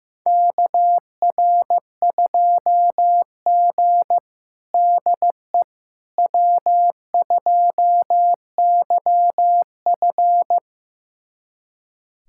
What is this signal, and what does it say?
Morse code, heard as a keyed tone, about 16 wpm, 700 Hz. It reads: KR2G DE W2YF
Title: USA Callsign DE Callsign - 4 Characters - 15wpm